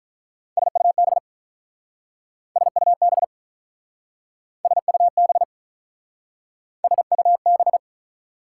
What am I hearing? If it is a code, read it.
Morse code, heard as a keyed tone, about 44 wpm, 700 Hz. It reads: SUB SUB SUB SUB